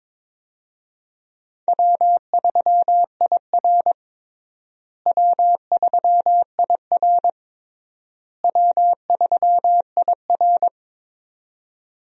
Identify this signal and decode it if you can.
Morse code, heard as a keyed tone, about 22 wpm, 700 Hz. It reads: W3IR W3IR W3IR
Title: USA Call Signs - 4 Characters - Repeated 3x - 22wpm